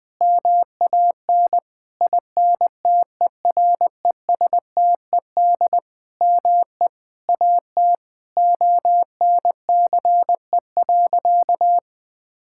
Morse code, heard as a keyed tone, about 20 wpm, 700 Hz. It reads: MAN INTERESTED ME AT ONCE.
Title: Sentences from Top 1000 Words - Review Part 6 - 20wpm